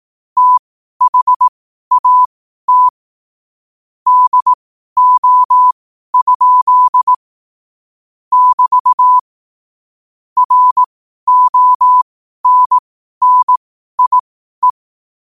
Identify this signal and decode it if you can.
Morse code, heard as a keyed tone, about 18 wpm, spaced as for 12 wpm, 1000 Hz. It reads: THAT DO? <BT> RONNIE
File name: qotdfiles / 12wpm